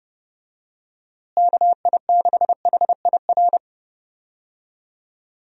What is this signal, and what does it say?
Morse code, heard as a keyed tone, about 30 wpm, 700 Hz. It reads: KI6HIR
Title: USA Call Signs - 6 Characters - Repeated 3x - 30wpm